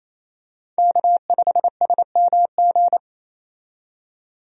Morse code, heard as a keyed tone, about 28 wpm, 700 Hz. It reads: K5SMG
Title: USA Call Signs - 5 Characters - 28wpm